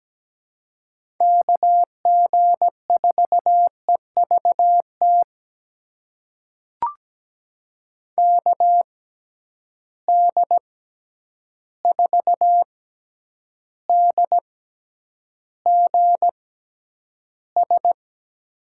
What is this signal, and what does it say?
Morse code, heard as a keyed tone, about 17 wpm, spaced as for 9 wpm, 700 Hz. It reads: KG4EVT KD4DGS
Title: USA Call Signs - 6x Character Spacing - 17wpm